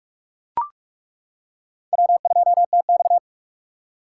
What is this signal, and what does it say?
Morse code, heard as a keyed tone, about 42 wpm, 700 Hz. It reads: W2TX
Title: K1USN SST QSO Practice - 45wpm